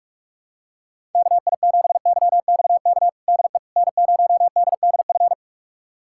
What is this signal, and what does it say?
Morse code, heard as a keyed tone, about 41 wpm, 700 Hz. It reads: KI7YXK DE N0DDF
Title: USA Callsign DE Callsign 45wpm